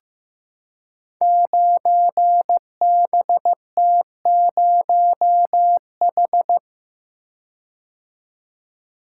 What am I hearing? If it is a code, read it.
Morse code, heard as a keyed tone, about 15 wpm, 700 Hz. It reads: 9BT0H
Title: Code Groups - 5 Letter:Numbers - 15wpm